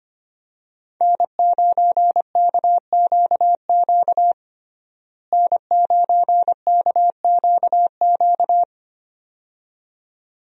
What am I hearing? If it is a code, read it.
Morse code, heard as a keyed tone, about 25 wpm, 700 Hz. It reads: N9KQQ N9KQQ
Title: USA Call Signs - 5 Characters - Repeated 2x - 25wpm